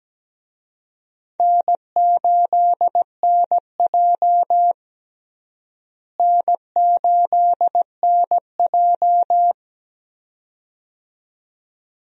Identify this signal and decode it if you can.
Morse code, heard as a keyed tone, about 17 wpm, 700 Hz. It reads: N8NJ N8NJ
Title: USA Call Signs - 4 Characters - Repeated 2x - 17wpm